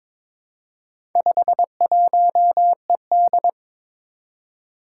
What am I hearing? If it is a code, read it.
Morse code, heard as a keyed tone, about 23 wpm, 700 Hz. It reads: 51ED
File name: practice / 22wpm / Four Letter-Number Combinations - 22wpm